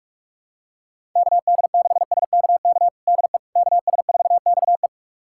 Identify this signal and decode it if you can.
Morse code, heard as a keyed tone, about 46 wpm, 700 Hz. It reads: KD6SKK DE KS4XE
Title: USA Callsign DE Callsign 45wpm